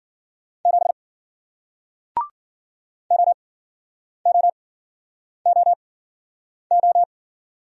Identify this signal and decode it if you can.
Morse code, heard as a keyed tone, about 50 wpm, spaced as for 9 wpm, 700 Hz. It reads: 7 OOOO
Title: Speed-Racing Single Letter-Number 40wpm